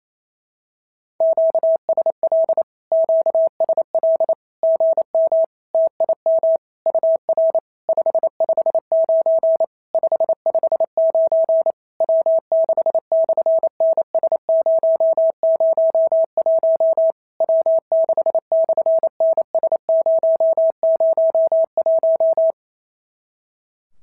Morse code, heard as a keyed tone, about 28 wpm, 650 Hz. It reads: QSL QSL GM TIM UR 559 559 W6/NS001 W6/NS001
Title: SOTA QSO Practice 28wpm